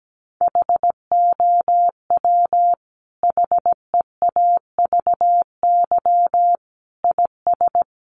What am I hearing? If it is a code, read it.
Morse code, heard as a keyed tone, about 17 wpm, 700 Hz. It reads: HOW HEAVY IS
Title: Sentences from Top 1100 Words 17wpm